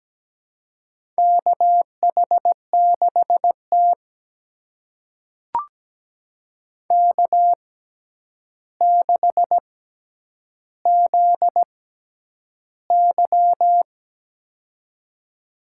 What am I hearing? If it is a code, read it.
Morse code, heard as a keyed tone, about 17 wpm, spaced as for 9 wpm, 700 Hz. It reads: KH6T K6ZY
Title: USA Call Signs - 4 Characters - 6x Character Spacing - 17wpm